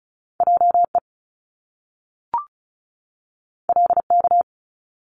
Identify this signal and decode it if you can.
Morse code, heard as a keyed tone, about 36 wpm, 700 Hz. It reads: JE LK
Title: Top Two Letter Combinations in Words 35wpm